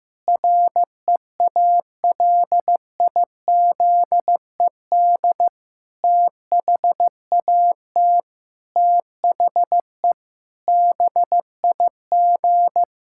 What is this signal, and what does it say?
Morse code, heard as a keyed tone, about 15 wpm, 700 Hz. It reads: REALIZED THAT THE BIG
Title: Pangram Challenge v2 - 15wpm